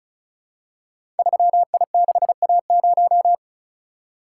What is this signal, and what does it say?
Morse code, heard as a keyed tone, about 36 wpm, 700 Hz. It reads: 3I6A0
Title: Code Groups - 5 Letter:Numbers - 35wpm